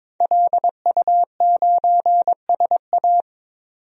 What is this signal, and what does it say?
Morse code, heard as a keyed tone, about 22 wpm, 700 Hz. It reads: LU9SA
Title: DXCC-100 Call Signs - 5 Characters - 22wpm